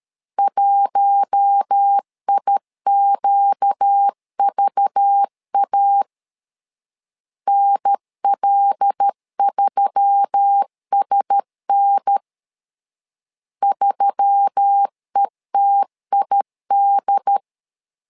Morse code, heard as a keyed tone, about 13 wpm, 800 Hz. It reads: 1IQVA NL3SN 3ETID
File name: LEZIONI / 5 LBVC / 5 LBV 12wpm 1855